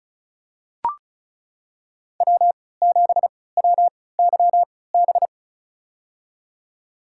Morse code, heard as a keyed tone, about 34 wpm, spaced as for 20 wpm, 700 Hz. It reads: W7WYB
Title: USA Call Signs - 5 Characters - 3x Character Spacing - 35wpm